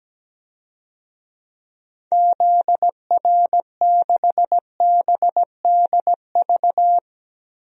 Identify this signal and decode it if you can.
Morse code, heard as a keyed tone, about 17 wpm, 700 Hz. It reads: ZR6BDV
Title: DXCC-100 Call Signs - 6 Characters - Repeated 2x - 17wpm